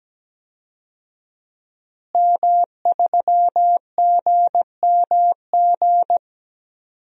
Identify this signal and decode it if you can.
Morse code, heard as a keyed tone, about 17 wpm, 700 Hz. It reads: M3GMG
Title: DXCC-100 Call Signs - 5 Characters - Repeated 3x - 17wpm